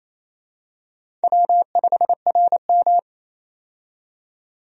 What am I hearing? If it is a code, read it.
Morse code, heard as a keyed tone, about 27 wpm, 700 Hz. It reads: W5RM
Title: USA Call Signs - 4 Characters - 28wpm